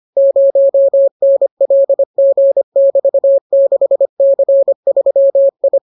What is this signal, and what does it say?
Morse code, heard as a keyed tone, about 25 wpm, 550 Hz. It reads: 0NLG<BT>6C3I